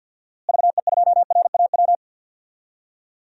Morse code, heard as a keyed tone, about 46 wpm, 700 Hz. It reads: VE2RAW
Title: DXCC-100 Call Signs - 6 Characters - Repeated 3x - 50wpm